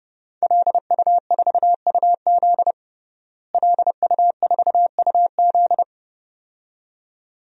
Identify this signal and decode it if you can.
Morse code, heard as a keyed tone, about 30 wpm, 700 Hz. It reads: LU4UZ LU4UZ
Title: DXCC-100 Call Signs - 5 Characters - Repeated 2x - 30wpm